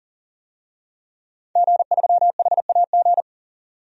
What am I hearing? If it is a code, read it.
Morse code, heard as a keyed tone, about 40 wpm, 700 Hz. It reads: G3HAG
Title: DXCC-100 Call Signs - 5 Characters - 40wpm